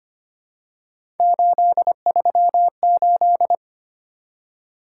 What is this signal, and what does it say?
Morse code, heard as a keyed tone, about 25 wpm, 700 Hz. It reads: 838
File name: Three Letter-Number Combinations - 25wpm